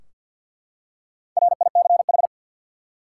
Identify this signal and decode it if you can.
Morse code, heard as a keyed tone, about 52 wpm, 700 Hz. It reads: RICH
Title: Four Letter Words - 50wpm